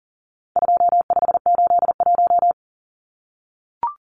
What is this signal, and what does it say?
Morse code, heard as a keyed tone, about 40 wpm, 700 Hz. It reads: 2581
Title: Numbers - 4-Digit 40wpm